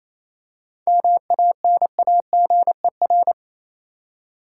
Morse code, heard as a keyed tone, about 28 wpm, 700 Hz. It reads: MANAGER
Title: Seven Letter Words - 28wpm